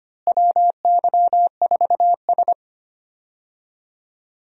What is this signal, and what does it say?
Morse code, heard as a keyed tone, about 25 wpm, 700 Hz. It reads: WY4S
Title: USA Call Signs - 4 Characters - Repeated 2x - 25wpm